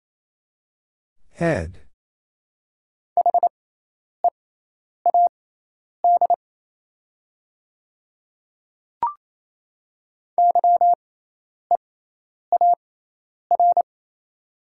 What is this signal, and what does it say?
Morse code, heard as a keyed tone, about 28 wpm, spaced as for 9 wpm, 700 Hz. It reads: THEAD YEAR
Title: 6x Character Spacing - Top 500 Words - 28wpm